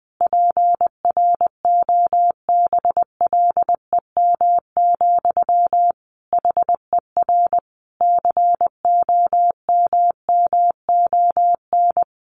Morse code, heard as a keyed tone, about 20 wpm, 700 Hz. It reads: PROBLEM, HER COMMON
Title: Sentences from Top 1200 Words 20wpm